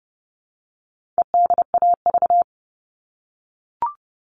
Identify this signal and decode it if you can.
Morse code, heard as a keyed tone, about 30 wpm, 700 Hz. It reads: EDAV
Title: All Cut Numbers - 4-Digit 30wpm